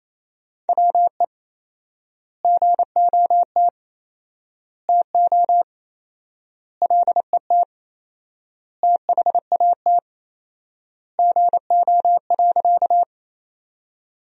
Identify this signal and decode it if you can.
Morse code, heard as a keyed tone, about 27 wpm, 700 Hz. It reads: WE GOT TO LET THAT GO.
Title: 4x Word-Spacing - Sentences from Top 300 Words - 28wpm